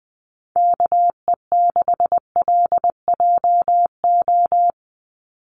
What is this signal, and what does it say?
Morse code, heard as a keyed tone, about 20 wpm, 700 Hz. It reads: KE6LJO
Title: USA Call Signs 20wpm